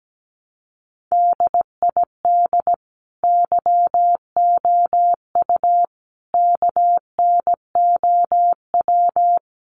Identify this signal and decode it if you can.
Morse code, heard as a keyed tone, about 17 wpm, 700 Hz. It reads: DID YOU KNOW
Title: Sentences from Top 100 Words 17wpm